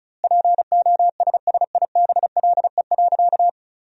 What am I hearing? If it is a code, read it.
Morse code, heard as a keyed tone, about 36 wpm, 700 Hz. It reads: POSSIBLE.
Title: 2x Word-Spacing - Sentences from Top 500 Words - 35wpm